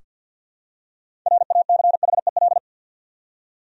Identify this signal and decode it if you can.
Morse code, heard as a keyed tone, about 45 wpm, 700 Hz. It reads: RACHEL
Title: USA Common Names - 50wpm